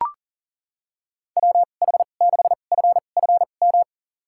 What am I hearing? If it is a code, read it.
Morse code, heard as a keyed tone, about 40 wpm, spaced as for 29 wpm, 700 Hz. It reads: WH6FFM
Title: USA Call Signs - 6 Characters - 2x Character Spacing - 40wpm